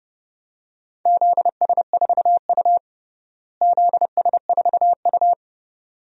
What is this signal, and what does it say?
Morse code, heard as a keyed tone, about 30 wpm, 700 Hz. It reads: ZS4U ZS4U
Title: DXCC-100 Call Signs - 4 Characters - Repeated 2x - 30wpm